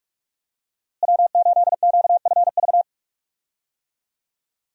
Morse code, heard as a keyed tone, about 46 wpm, 700 Hz. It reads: W8QFV
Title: USA Call Signs - 5 Characters - Repeated 3x - 45wpm